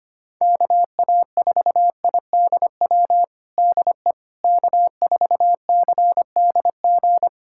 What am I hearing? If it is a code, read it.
Morse code, heard as a keyed tone, about 25 wpm, 700 Hz. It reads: KA4IDW DE K4CDG
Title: USA Callsign DE Callsign 25wpm